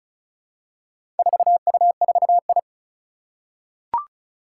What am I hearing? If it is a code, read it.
Morse code, heard as a keyed tone, about 34 wpm, 700 Hz. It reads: 4U4I
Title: Four Challenging Letter-Number Combinations 35wpm